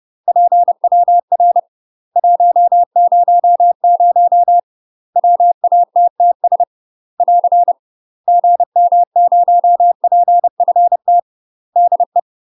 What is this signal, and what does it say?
Morse code, heard as a keyed tone, about 30 wpm, 700 Hz. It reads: PWR 100 WATTS <AR> GM0PFT DE
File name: UK MASTER QSOs TRAINING / 30WPM / UK MASTER QSOs 0004